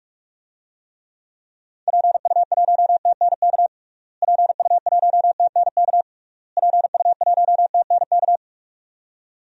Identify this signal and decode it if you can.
Morse code, heard as a keyed tone, about 46 wpm, 700 Hz. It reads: PU1TNK PU1TNK PU1TNK
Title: DXCC-100 Call Signs - 6 Characters - Repeated 3x - 45wpm